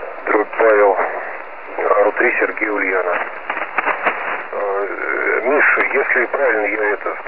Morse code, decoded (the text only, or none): none